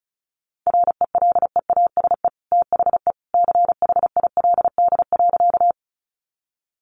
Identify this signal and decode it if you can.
Morse code, heard as a keyed tone, about 34 wpm, 700 Hz. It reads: RELEASE THE CHILD.
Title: Sentences from Top 1100 Words 35wpm